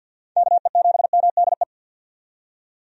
Morse code, heard as a keyed tone, about 50 wpm, 700 Hz. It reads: KE7MDE
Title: USA Call Signs - 6 Characters - 50wpm